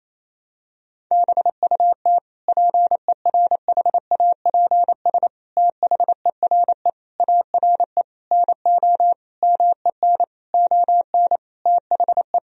Morse code, heard as a keyed tone, about 28 wpm, 700 Hz. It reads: BUT PERHAPS THERE ARE NO MEN ON THE
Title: Sentences from Top 1000 Words - Review Part 3 - 28wpm